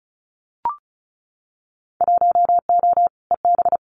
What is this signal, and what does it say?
Morse code, heard as a keyed tone, about 34 wpm, 700 Hz. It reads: 1O EB
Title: Winter Field Day Exchanges 35wpm